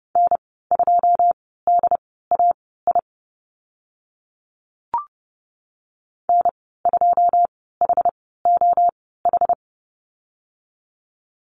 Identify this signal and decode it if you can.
Morse code, heard as a keyed tone, about 30 wpm, spaced as for 17 wpm, 700 Hz. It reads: N2DAI N2HOH
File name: Lesson 15g - Number 5 - Call Signs 30wpm